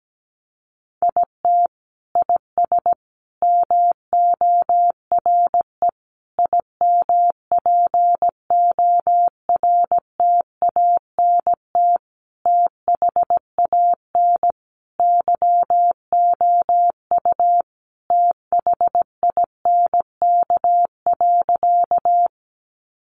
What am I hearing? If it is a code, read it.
Morse code, heard as a keyed tone, about 17 wpm, 700 Hz. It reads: IT IS MORE IMPORTANT THAN YOU THINK.
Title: Sentences from Top 600 Words 17wpm